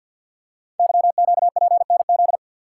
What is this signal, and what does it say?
Morse code, heard as a keyed tone, about 51 wpm, 690 Hz. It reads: YQPNZ
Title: Code Groups - 5 Letters - 50wpm